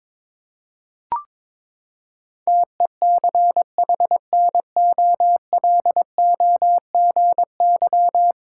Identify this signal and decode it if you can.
Morse code, heard as a keyed tone, about 22 wpm, 700 Hz. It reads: TECHNOLOGY
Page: morse-code-ninja-practice-22wpm